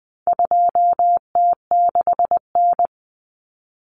Morse code, heard as a keyed tone, about 20 wpm, 700 Hz. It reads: 2T6N
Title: Common Cut Numbers - 4-Digit 20wpm